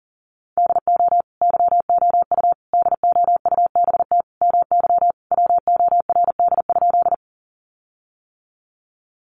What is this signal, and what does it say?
Morse code, heard as a keyed tone, about 37 wpm, 700 Hz. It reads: DO YOU DOUBT MY WORD?
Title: Sentences from Top 1500 Words 40wpm